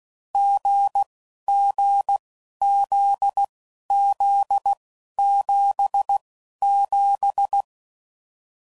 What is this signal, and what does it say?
Morse code, heard as a keyed tone, about 16 wpm, spaced as for 11 wpm, 800 Hz. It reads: GGZZ77